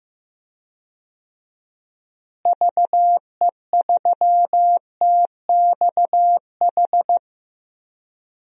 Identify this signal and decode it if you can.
Morse code, heard as a keyed tone, about 15 wpm, 700 Hz. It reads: VE3TXH
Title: DXCC-100 Call Signs - 6 Characters - Repeated 3x - 15wpm